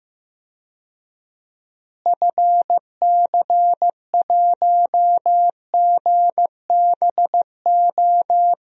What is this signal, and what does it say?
Morse code, heard as a keyed tone, about 15 wpm, 700 Hz. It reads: FC1GBO
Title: DXCC-100 Call Signs - 6 Characters - Repeated 2x - 15wpm